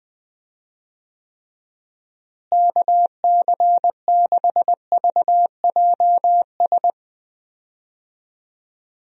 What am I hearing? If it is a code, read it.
Morse code, heard as a keyed tone, about 20 wpm, 700 Hz. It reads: KC6VJS
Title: USA Call Signs - 6 Characters - Repeated 2x - 20wpm